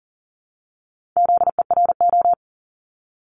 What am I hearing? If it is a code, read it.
Morse code, heard as a keyed tone, about 39 wpm, 700 Hz. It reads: ZERO
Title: Multi-letter Phonemes Repeated 3x 40wpm